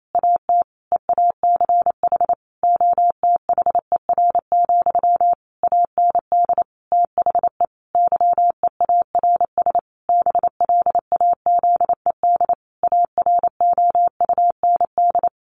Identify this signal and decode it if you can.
Morse code, heard as a keyed tone, about 28 wpm, 700 Hz. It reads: AT EACH OTHER, AND THE YEARS BLAZED AROUND
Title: A Sound Of Thunder 28wpm